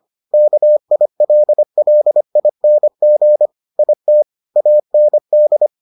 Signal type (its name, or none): Morse code